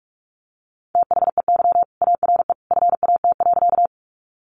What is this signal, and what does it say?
Morse code, heard as a keyed tone, about 44 wpm, 700 Hz. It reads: THEY ARE FAT.
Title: Sentences from Top 600 Words 45wpm